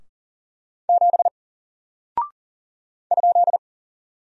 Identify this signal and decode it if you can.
Morse code, heard as a keyed tone, about 40 wpm, 700 Hz. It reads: 7 ?
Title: Single Character (Letter:Number:Punctuation:Prosign) 40wpm